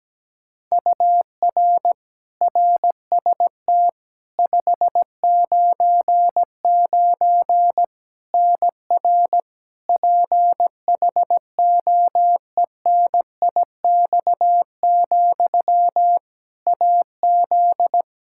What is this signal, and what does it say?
Morse code, heard as a keyed tone, about 17 wpm, 700 Hz. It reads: UR RST 599 NR PHOENIX, AZ
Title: Practice QSO - Exchange 1 - Beginning - 17wpm